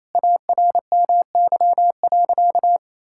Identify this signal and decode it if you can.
Morse code, heard as a keyed tone, about 28 wpm, 700 Hz. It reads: ARMY.
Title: Sentences from Top 1000 Words - Review Part 6 - 28wpm